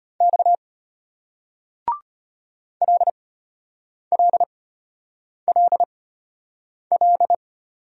Morse code, heard as a keyed tone, about 33 wpm, spaced as for 8 wpm, 700 Hz. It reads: X LLLL